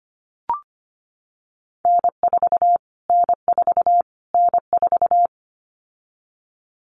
Morse code, heard as a keyed tone, about 25 wpm, 700 Hz. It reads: N4 N4 N4
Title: Callsign Prefixes for Top 25 DXCC - Repeated 3x - 25wpm